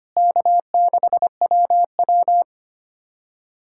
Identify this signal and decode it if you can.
Morse code, heard as a keyed tone, about 25 wpm, 700 Hz. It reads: K6WW
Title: USA Callsign DE Callsign - 4 Characters - 25wpm